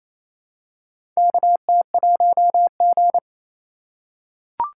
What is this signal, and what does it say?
Morse code, heard as a keyed tone, about 27 wpm, 700 Hz. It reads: KT1G
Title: USA Call Signs - 3x Character Spacing - 28wpm